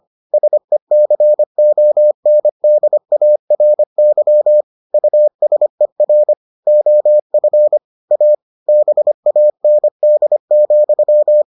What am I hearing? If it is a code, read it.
Morse code, heard as a keyed tone, about 25 wpm, 600 Hz. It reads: SECONDARY USER OF A BAND,